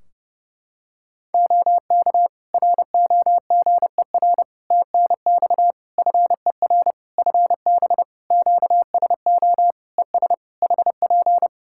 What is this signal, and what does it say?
Morse code, heard as a keyed tone, about 30 wpm, 700 Hz. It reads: OK ROGER TNX FER FB QSO ES HP
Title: Practice QSO - Exchange 4 - Ending - 30wpm